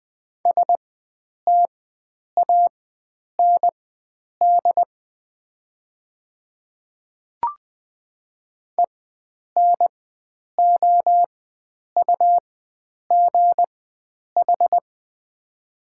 Morse code, heard as a keyed tone, about 20 wpm, spaced as for 9 wpm, 700 Hz. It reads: STAND ENOUGH
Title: 4x Character Spacing - Top 500 Words - 20wpm